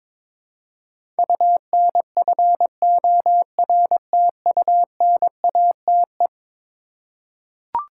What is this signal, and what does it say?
Morse code, heard as a keyed tone, about 22 wpm, 700 Hz. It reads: UNFORTUNATE
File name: practice / 22wpm / Eleven Letter Words - 22wpm